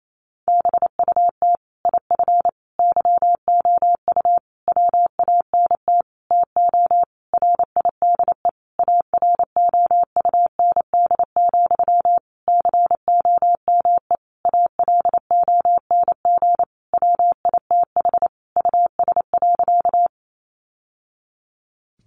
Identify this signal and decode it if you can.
Morse code, heard as a keyed tone, about 28 wpm, 700 Hz. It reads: BUT IF YOU WANT TO RIDE AROUND, COME ALONG WITH US.
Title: Sentences from Top 1200 Words 28wpm